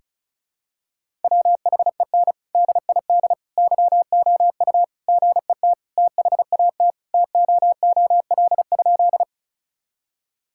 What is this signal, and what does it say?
Morse code, heard as a keyed tone, about 36 wpm, 700 Hz. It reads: WHEN DID YOU GET THAT TOOL?
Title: Sentences from Top 1000 Words - Review Part 6 - 35wpm